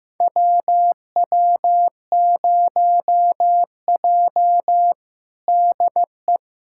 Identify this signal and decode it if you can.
Morse code, heard as a keyed tone, about 15 wpm, 700 Hz. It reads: WW0J DE
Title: USA Callsign DE Callsign - 4 Characters - 15wpm